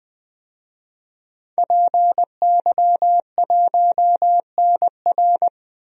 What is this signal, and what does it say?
Morse code, heard as a keyed tone, about 20 wpm, 700 Hz. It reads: PY1NR
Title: DXCC-100 Call Signs - 5 Characters - 20wpm